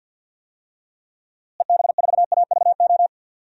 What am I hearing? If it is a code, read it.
Morse code, heard as a keyed tone, about 51 wpm, 700 Hz. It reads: EB4AVO
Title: DXCC-100 Call Signs - 6 Characters - Repeated 3x - 50wpm